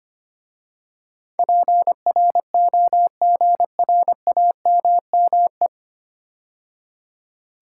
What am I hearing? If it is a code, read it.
Morse code, heard as a keyed tone, about 25 wpm, 690 Hz. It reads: PROGRAMME
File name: practice / 25wpm / 9 Letter Words - First 3 Letters Given - 25wpm